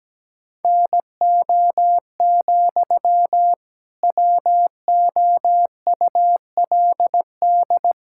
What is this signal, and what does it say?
Morse code, heard as a keyed tone, about 17 wpm, 700 Hz. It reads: NO, WOULD
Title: Sentences from Top 1600 Words 17wpm